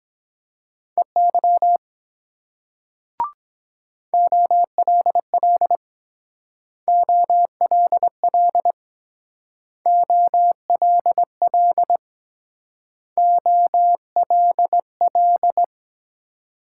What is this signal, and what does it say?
Morse code, heard as a keyed tone, about 21 wpm, 700 Hz. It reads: EY OLL OLL OLL OLL